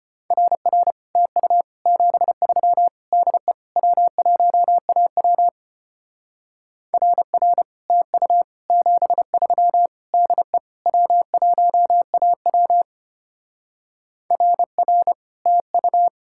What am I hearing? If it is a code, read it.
Morse code, heard as a keyed tone, about 32 wpm, 700 Hz. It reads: RR TU 73 DE W1AW RR TU 73 DE W1AW RR TU